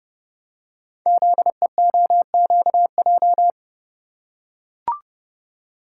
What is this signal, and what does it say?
Morse code, heard as a keyed tone, about 30 wpm, 690 Hz. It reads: ZEOQJ
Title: Code Groups - 5 Letters - 30wpm